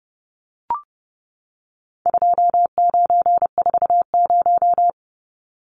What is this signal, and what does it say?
Morse code, heard as a keyed tone, about 30 wpm, 700 Hz. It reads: 2940
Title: Numbers - 4-Digit 30wpm